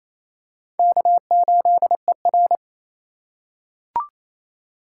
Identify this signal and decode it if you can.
Morse code, heard as a keyed tone, about 28 wpm, 700 Hz. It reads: K8ER
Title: USA Call Signs - 4 Characters - 2x Character Spacing - 28wpm